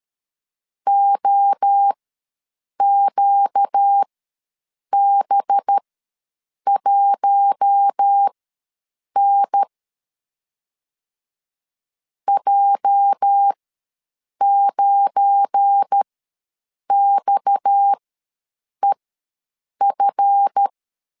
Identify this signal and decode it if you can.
Morse code, heard as a keyed tone, about 13 wpm, spaced as for 7 wpm, 800 Hz. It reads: OQB1N J9XEF
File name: LEZIONI / 8 YJ45 / 8 YJ ws 12wpm 3342